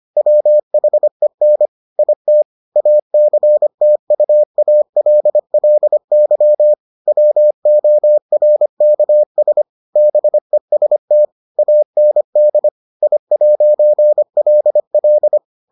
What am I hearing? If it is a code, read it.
Morse code, heard as a keyed tone, about 25 wpm, 600 Hz. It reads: WHEN IT ACTUALLY WORKS BEST AND I'LL